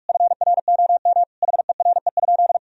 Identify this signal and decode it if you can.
Morse code, heard as a keyed tone, about 46 wpm, 700 Hz. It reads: FROM HERE?
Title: Sentences from Top 1000 Words - Review Part 4 - 45wpm